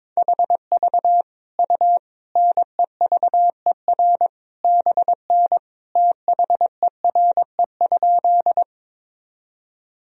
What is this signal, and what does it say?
Morse code, heard as a keyed tone, about 21 wpm, 700 Hz. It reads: HV U NEVER BN THERE?
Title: Sentences from Top 300 Words and Abbreviations 22wpm